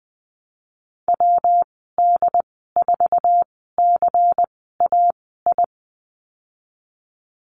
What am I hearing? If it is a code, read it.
Morse code, heard as a keyed tone, about 20 wpm, spaced as for 15 wpm, 700 Hz. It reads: WD4CAI
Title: 173 Lesson 28g - Number 9 - Call Signs 20wpm